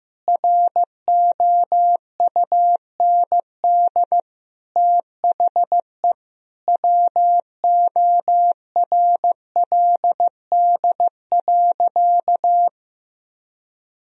Morse code, heard as a keyed tone, about 15 wpm, 700 Hz. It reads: ROUND THE WORLD.
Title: Sentences from Top 1000 Words - Review Part 1 - 15wpm